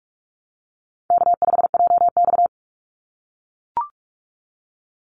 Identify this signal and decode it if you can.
Morse code, heard as a keyed tone, about 44 wpm, 700 Hz. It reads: K5JX
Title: Easy Call Signs - 45wpm